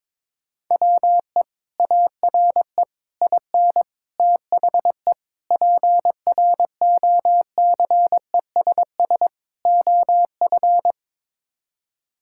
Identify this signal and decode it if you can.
Morse code, heard as a keyed tone, about 22 wpm, 700 Hz. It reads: WE ARE IN THE PROCESS OF
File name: practice / 22wpm / Sets of 6 Words - 22wpm